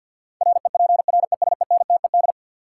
Morse code, heard as a keyed tone, about 53 wpm, 700 Hz. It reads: REPRESENTED
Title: Sets of 5 Words - New Word Drill - 50wpm